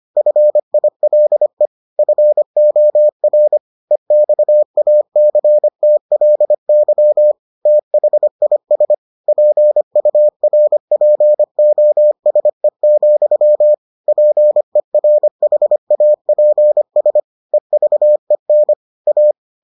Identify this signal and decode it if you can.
Morse code, heard as a keyed tone, about 25 wpm, 600 Hz. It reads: FILE FOR EXACTLY THIS PURPOSE, PERHAPS EVEN A